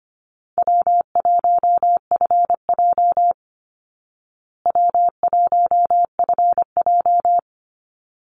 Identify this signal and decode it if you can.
Morse code, heard as a keyed tone, about 25 wpm, 700 Hz. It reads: W1FJ W1FJ
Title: Easy Call Signs Repeated 2x - 25wpm